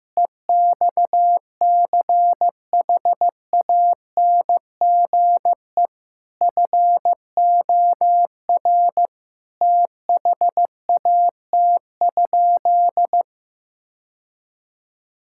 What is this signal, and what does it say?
Morse code, heard as a keyed tone, about 15 wpm, 700 Hz. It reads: EXCHANGE FOR THAT?
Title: Sentences from Top 1000 Words - Review Part 6 - 15wpm